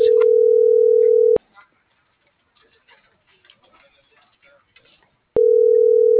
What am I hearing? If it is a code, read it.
Ringback tone (North America), 440 + 480 Hz.